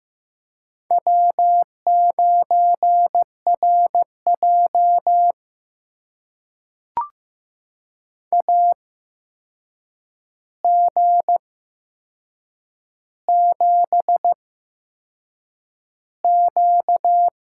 Morse code, heard as a keyed tone, about 15 wpm, spaced as for 6 wpm, 700 Hz. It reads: W9RJ AG7Q
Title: USA Call Signs - 4 Characters - 8x Character Spacing - 15wpm